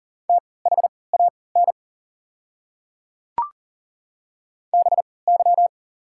Morse code, heard as a keyed tone, about 39 wpm, spaced as for 23 wpm, 690 Hz. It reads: THAN BY
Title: 3x Character Spacing - Top 200 Words - 40wpm